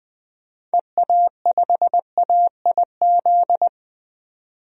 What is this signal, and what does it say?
Morse code, heard as a keyed tone, about 21 wpm, 700 Hz. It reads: EA5AIZ
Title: DXCC-100 Call Signs - 6 Characters - 20wpm